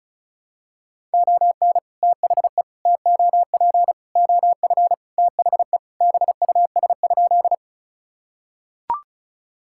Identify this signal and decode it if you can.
Morse code, heard as a keyed tone, about 34 wpm, 700 Hz. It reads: ON THE TOP OF THE BUS?
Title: Sentences from Top 1000 Words - Review Part 6 - 35wpm